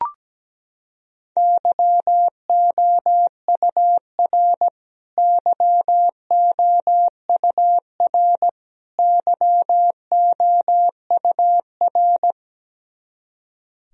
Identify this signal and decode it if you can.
Morse code, heard as a keyed tone, about 17 wpm, 700 Hz. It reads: YOUR YOUR YOUR
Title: Top 100 Words Repeated 17wpm